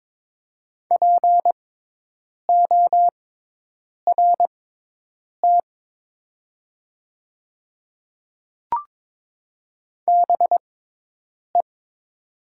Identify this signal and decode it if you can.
Morse code, heard as a keyed tone, about 22 wpm, spaced as for 8 wpm, 700 Hz. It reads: PORT BE
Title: 6x Character Spacing - Top 200 Words - 22wpm